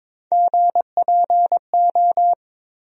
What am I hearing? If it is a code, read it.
Morse code, heard as a keyed tone, about 22 wpm, 700 Hz. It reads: GPO
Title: USA Call Sign Postfixes - 3 Letters 22wpm